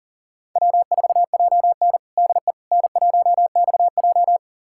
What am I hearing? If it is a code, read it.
Morse code, heard as a keyed tone, about 38 wpm, 700 Hz. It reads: W4JN DE N1XJ
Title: USA Callsign DE Callsign - 4 Characters - 40wpm